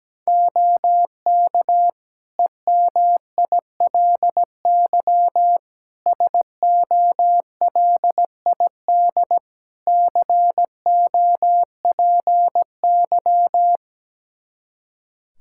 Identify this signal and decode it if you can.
Morse code, heard as a keyed tone, about 17 wpm, 700 Hz. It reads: OK EMILY SOLID COPY
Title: Practice QSO - Exchange 3 - More Info - 17wpm